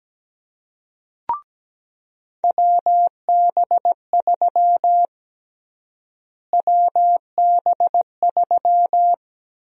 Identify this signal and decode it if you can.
Morse code, heard as a keyed tone, about 17 wpm, 700 Hz. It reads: WB3 WB3
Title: USA Call Sign Prefixes - Repeated 2x - 17wpm